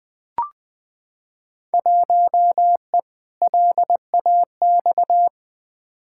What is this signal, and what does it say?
Morse code, heard as a keyed tone, about 21 wpm, 700 Hz. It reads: 1E LAX
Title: Field Day Exchanges - 20wpm